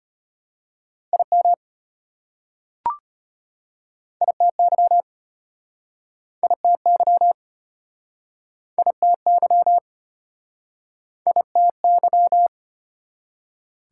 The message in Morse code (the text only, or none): IM ITY ITY ITY ITY